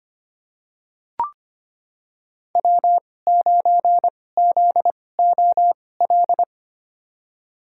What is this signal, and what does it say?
Morse code, heard as a keyed tone, about 25 wpm, spaced as for 18 wpm, 700 Hz. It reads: W9ZOL
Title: USA Call Signs - 5 Characters - 2x Character Spacing - 25wpm